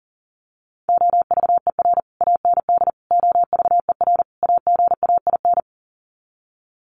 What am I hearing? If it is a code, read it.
Morse code, heard as a keyed tone, about 40 wpm, 700 Hz. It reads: OVER AND OVER AGAIN
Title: Sets of 4 Words 40wpm